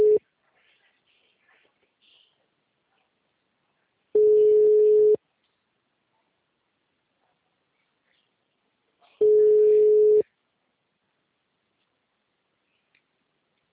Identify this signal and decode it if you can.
Ringback tone (Europe, 425 Hz).